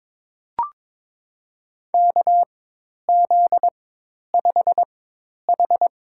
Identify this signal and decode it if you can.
Morse code, heard as a keyed tone, about 21 wpm, spaced as for 10 wpm, 700 Hz. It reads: KZ5H
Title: USA Call Signs - 4 Characters - 4x Character Spacing - 22wpm